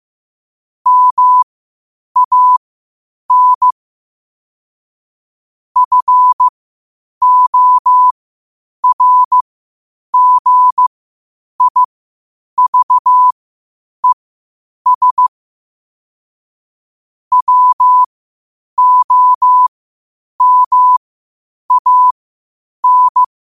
Morse code, heard as a keyed tone, about 15 wpm, spaced as for 8 wpm, 1000 Hz. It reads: MAN FORGIVES WOMAN
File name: qotdfiles / 8wpm